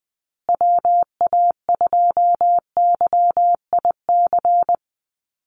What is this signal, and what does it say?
Morse code, heard as a keyed tone, about 20 wpm, 700 Hz. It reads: WA2YIC
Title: USA Call Signs Repeated 3x 20wpm